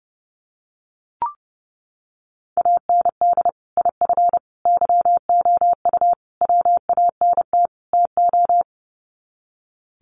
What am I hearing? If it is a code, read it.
Morse code, heard as a keyed tone, about 28 wpm, 700 Hz. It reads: AND IF YOU WANT TO
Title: Sets of 5 Words Encore 30wpm